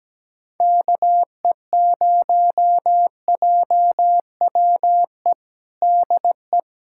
Morse code, heard as a keyed tone, about 17 wpm, 700 Hz. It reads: KE0JWE DE
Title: USA Callsign DE Callsign 17wpm